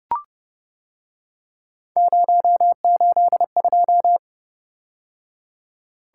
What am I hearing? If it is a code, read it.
Morse code, heard as a keyed tone, about 29 wpm, 700 Hz. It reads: 082